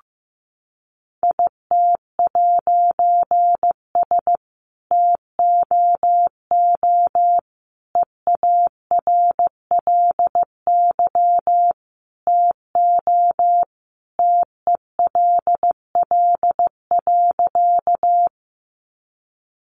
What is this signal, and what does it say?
Morse code, heard as a keyed tone, about 15 wpm, 700 Hz. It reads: IT'S TOO EARLY TO TELL.
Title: A Sound Of Thunder 15wpm